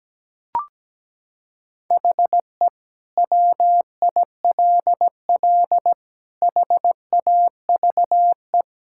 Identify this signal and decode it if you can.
Morse code, heard as a keyed tone, about 17 wpm, 700 Hz. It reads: HE WILL HAVE
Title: Sentences from Top 1000 Words - Review Part 6 - 17wpm